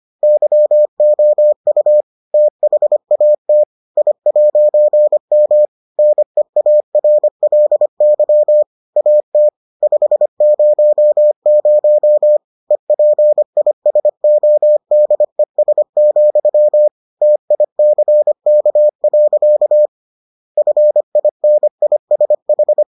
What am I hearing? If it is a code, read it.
Morse code, heard as a keyed tone, about 25 wpm, 600 Hz. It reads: YOU THAT I'M NEARLY AT 500 EPISODES, TICK. FINISH